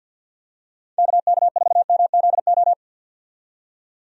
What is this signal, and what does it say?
Morse code, heard as a keyed tone, about 52 wpm, 700 Hz. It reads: KK4MZO
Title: USA Call Signs - 6 Characters - 50wpm